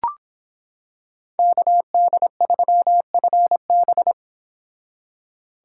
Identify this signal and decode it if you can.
Morse code, heard as a keyed tone, about 25 wpm, 700 Hz. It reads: KD3FB